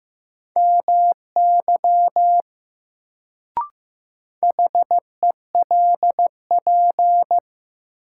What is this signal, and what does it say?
Morse code, heard as a keyed tone, about 15 wpm, 700 Hz. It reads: MY HELP
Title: Sets of 2 Words - New Word Drill - 15wpm